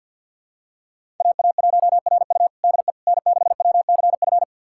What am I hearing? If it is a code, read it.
Morse code, heard as a keyed tone, about 51 wpm, 700 Hz. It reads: AA1RU DE N6WCF
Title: USA Callsign DE Callsign - 5 Characters - 50wpm